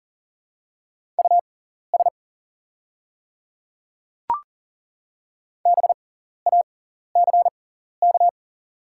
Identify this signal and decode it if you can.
Morse code, heard as a keyed tone, about 40 wpm, spaced as for 14 wpm, 700 Hz. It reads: US BACK
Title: 6x Character Spacing - Top 200 Words - 40wpm